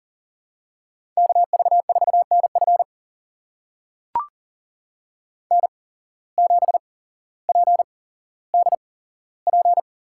Morse code, heard as a keyed tone, about 40 wpm, spaced as for 17 wpm, 700 Hz. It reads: KV4NF N7PDP
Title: USA Call Signs - 5 Characters - 8x Character Spacing - 40wpm